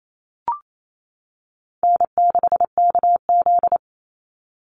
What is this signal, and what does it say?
Morse code, heard as a keyed tone, about 28 wpm, 700 Hz. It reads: N6KZ
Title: NA SOTA Chaser Callsigns 28wpm